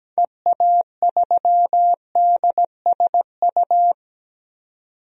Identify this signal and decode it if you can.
Morse code, heard as a keyed tone, about 17 wpm, 700 Hz. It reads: EA3DSU
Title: DXCC-100 Call Signs - 6 Characters - 17wpm